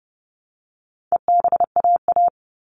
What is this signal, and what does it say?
Morse code, heard as a keyed tone, about 30 wpm, 700 Hz. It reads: EBAA
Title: All Cut Numbers - 4-Digit 30wpm